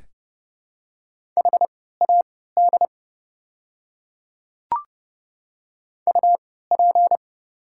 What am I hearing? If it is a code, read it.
Morse code, heard as a keyed tone, about 30 wpm, spaced as for 17 wpm, 700 Hz. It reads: HAD UP
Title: 3x Character Spacing - Top 100 Words - 30wpm